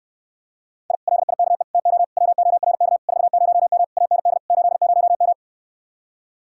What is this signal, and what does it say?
Morse code, heard as a keyed tone, about 68 wpm, 700 Hz. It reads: I LIKE TO WORK 40M AND 80M